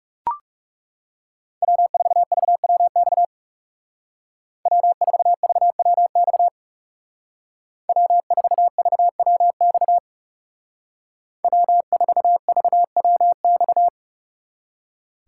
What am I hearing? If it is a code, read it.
Morse code, heard as a keyed tone, about 36 wpm, 700 Hz. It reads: W4VWX W4VWX W4VWX W4VWX
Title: Speed Racing - USA 1x3 Call Signs 30wpm